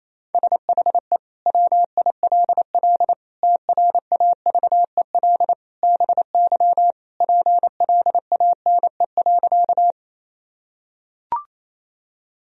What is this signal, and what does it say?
Morse code, heard as a keyed tone, about 28 wpm, 700 Hz. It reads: SHE WILL TRAVEL BY PLANE.
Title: Sentences from Top 1000 Words - Review Part 3 - 28wpm